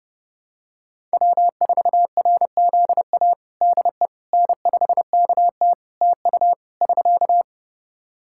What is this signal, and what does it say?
Morse code, heard as a keyed tone, about 29 wpm, 700 Hz. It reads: W4RZA DE N5KT TU <SK>
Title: Practice QSO - Exchange 4 - Ending - 30wpm